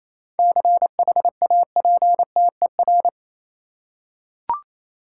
Morse code, heard as a keyed tone, about 28 wpm, 700 Hz. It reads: CHAPTER
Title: Seven Letter Words - 28wpm